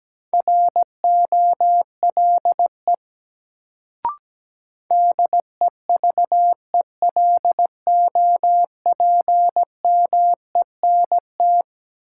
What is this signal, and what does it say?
Morse code, heard as a keyed tone, about 17 wpm, 700 Hz. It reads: ROLE DEVELOPMENT
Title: Sets of 3 Words - New Word Drill - 17wpm